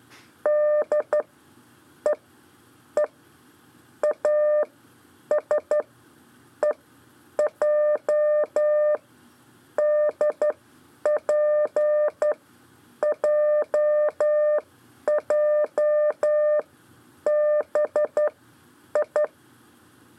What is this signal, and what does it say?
Morse code, heard as a keyed tone, about 12 wpm, spaced as for 8 wpm, 600 Hz. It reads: D E E AS EJ DPJJBI
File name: 4 LEZIONE CW 1